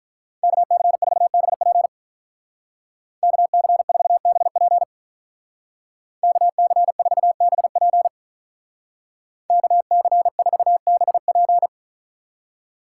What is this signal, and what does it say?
Morse code, heard as a keyed tone, about 43 wpm, 700 Hz. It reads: KC4BP KC4BP KC4BP KC4BP
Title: Speed Racing - USA 2x2 Call Signs 35wpm